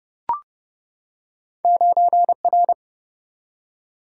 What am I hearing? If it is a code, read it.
Morse code, heard as a keyed tone, about 30 wpm, 700 Hz. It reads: 9R
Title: Single-Letter Single-Number 30wpm